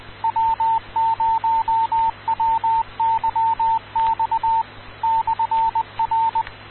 Morse code, heard as a keyed tone, about 20 wpm, 900 Hz, with radio noise. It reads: W0WYX /R
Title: Rocky Mountain Radio League Repeaters